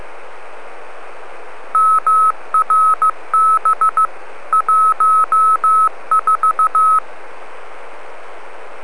Morse code, heard as a keyed tone, about 15 wpm, 1290 Hz. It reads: MRB 14